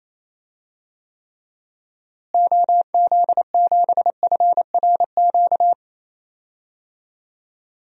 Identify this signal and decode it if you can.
Morse code, heard as a keyed tone, about 28 wpm, 700 Hz. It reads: OZ7FRQ
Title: DXCC-100 Call Signs - 6 Characters - Repeated 3x - 28wpm